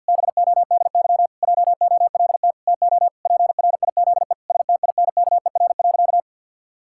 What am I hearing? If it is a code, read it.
Morse code, heard as a keyed tone, about 51 wpm, 700 Hz. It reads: BODY JOLT TO PRIZE STINKER.
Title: Pangram Challenge v2 - 50wpm